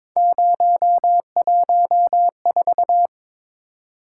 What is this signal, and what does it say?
Morse code, heard as a keyed tone, about 22 wpm, 700 Hz. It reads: 014
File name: practice / 22wpm / Three Letter-Number Combinations - 22wpm